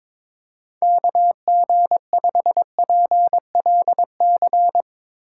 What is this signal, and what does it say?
Morse code, heard as a keyed tone, about 22 wpm, 700 Hz. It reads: KG5PLC
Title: Speed Racing - USA 2x3 Call Signs 22wpm